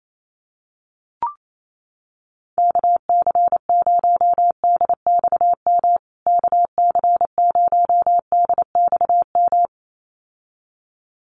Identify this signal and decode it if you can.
Morse code, heard as a keyed tone, about 28 wpm, 700 Hz. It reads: KC0DXM KC0DXM
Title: USA Call Signs Repeated 2x 28wpm